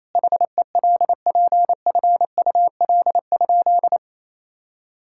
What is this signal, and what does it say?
Morse code, heard as a keyed tone, about 28 wpm, 700 Hz. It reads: HELPFUL?
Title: Sentences from Top 1000 Words - Review Part 6 - 28wpm